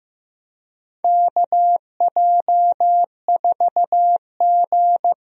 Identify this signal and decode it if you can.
Morse code, heard as a keyed tone, about 15 wpm, 700 Hz. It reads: KJ4G
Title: USA Call Signs - 4 Characters - 15wpm